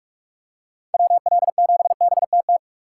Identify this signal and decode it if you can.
Morse code, heard as a keyed tone, about 46 wpm, 700 Hz. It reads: WL7BTT
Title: USA Call Signs - 6 Characters - 4x Character Spacing - 45wpm